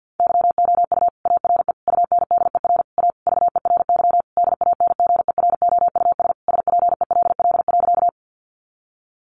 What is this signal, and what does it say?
Morse code, heard as a keyed tone, about 50 wpm, 700 Hz. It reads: YOU ARE UNDER A VERY DANGEROUS SPELL.
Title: Sentences from Top 1200 Words 50wpm